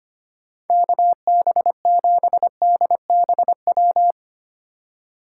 Morse code, heard as a keyed tone, about 25 wpm, 700 Hz. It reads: KB7DBW
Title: USA Call Signs - 6 Characters - 2x Character Spacing - 25wpm